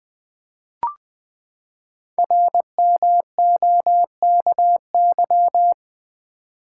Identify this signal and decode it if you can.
Morse code, heard as a keyed tone, about 20 wpm, 700 Hz. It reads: RMOKY
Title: Code Groups - 5 Letters - 20wpm